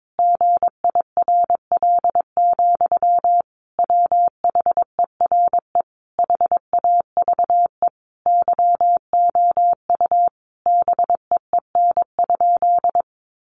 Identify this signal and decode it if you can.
Morse code, heard as a keyed tone, about 22 wpm, 700 Hz. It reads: GIRL, WHERE HAVE YOU BEEN?
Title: Sentences from Top 400 Words 22wpm